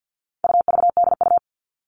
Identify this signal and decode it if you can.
Morse code, heard as a keyed tone, about 50 wpm, 700 Hz. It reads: UVDU
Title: All Cut Numbers - 4-Digit 50wpm